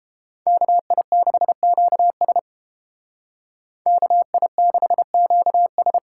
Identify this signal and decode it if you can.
Morse code, heard as a keyed tone, about 29 wpm, 700 Hz. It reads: KI6QS KI6QS